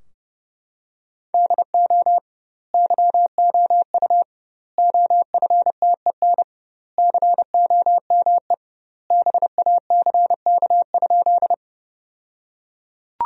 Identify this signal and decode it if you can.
Morse code, heard as a keyed tone, about 32 wpm, 700 Hz. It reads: DO YOU OFTEN COME BACK?
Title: 2x Word-Spacing - Sentences from Top 300 Words - 30wpm